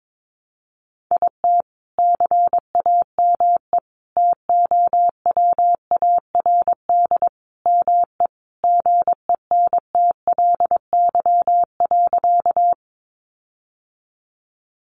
Morse code, heard as a keyed tone, about 22 wpm, 700 Hz. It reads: IT CAME TOWARD ME GENTLY.
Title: Sentences from Top 1500 Words 22wpm